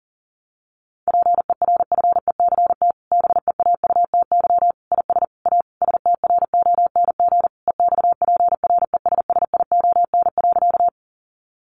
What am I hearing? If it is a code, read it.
Morse code, heard as a keyed tone, about 40 wpm, 700 Hz. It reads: PERFECT BEAUTY IS A STRONG EXPRESSION.
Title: Sentences from Top 1200 Words 40wpm